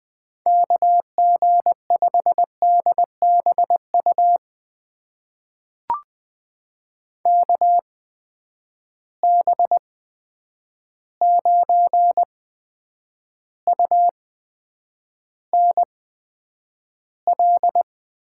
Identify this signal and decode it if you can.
Morse code, heard as a keyed tone, about 20 wpm, spaced as for 9 wpm, 700 Hz. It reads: KG5DBU KB9UNL
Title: USA Call Signs - 6 Characters - 8x Character Spacing - 20wpm